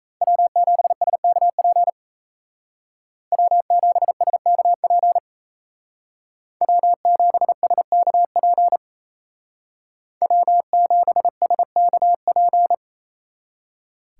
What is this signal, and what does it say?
Morse code, heard as a keyed tone, about 33 wpm, 700 Hz. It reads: W7SKP W7SKP W7SKP W7SKP
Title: Speed Racing - USA Call Signs 28wpm